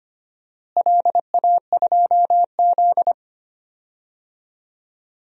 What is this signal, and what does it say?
Morse code, heard as a keyed tone, about 25 wpm, 700 Hz. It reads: LA2Z
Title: DXCC-100 Call Signs - 4 Characters - Repeated 3x - 25wpm